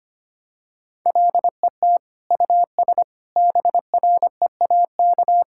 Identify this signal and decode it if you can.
Morse code, heard as a keyed tone, about 25 wpm, 700 Hz. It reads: LET US BREAK
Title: Sentences from Top 1000 Words - Review Part 6 - 25wpm